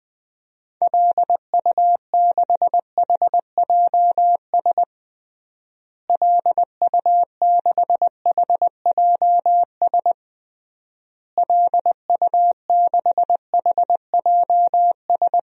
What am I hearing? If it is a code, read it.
Morse code, heard as a keyed tone, about 20 wpm, 700 Hz. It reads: LU6HJS LU6HJS LU6HJS
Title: DXCC-100 Call Signs - 6 Characters - Repeated 3x - 20wpm